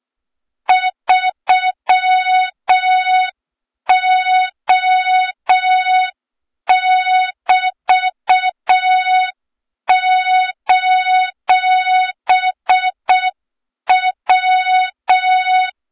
Morse code, heard as a keyed tone, about 6 wpm, 3700 Hz, with noise. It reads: 3O<BT>:W
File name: Morze-HA2MN / hang / 23 GYAKORLAT